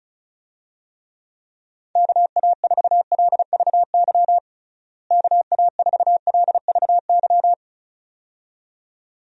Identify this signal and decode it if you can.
Morse code, heard as a keyed tone, about 36 wpm, 700 Hz. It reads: KA4LVY KA4LVY
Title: USA Call Signs - 6 Characters - Repeated 2x - 35wpm